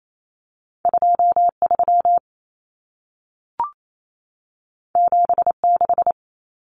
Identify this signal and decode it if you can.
Morse code, heard as a keyed tone, about 28 wpm, 700 Hz. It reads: 23 76
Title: Numbers - 2-Digit 28wpm